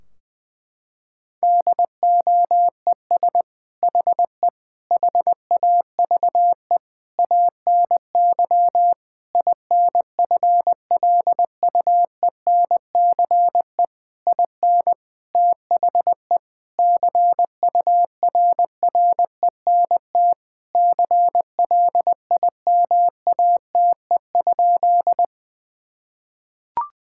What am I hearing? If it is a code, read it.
Morse code, heard as a keyed tone, about 21 wpm, 700 Hz. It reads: DOES HE HAVE ANY INFLUENCE IN THE CURRENT CLIMATE?
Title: Sentences from Top 1600 Words 20wpm